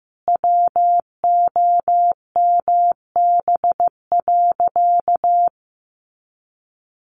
Morse code, heard as a keyed tone, about 15 wpm, 700 Hz. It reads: WOMB.
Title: A Sound Of Thunder 15wpm